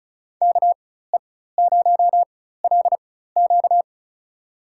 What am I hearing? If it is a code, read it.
Morse code, heard as a keyed tone, about 36 wpm, spaced as for 16 wpm, 700 Hz. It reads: KE0LQ
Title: USA Call Signs - 4x Character Spacing - 35wpm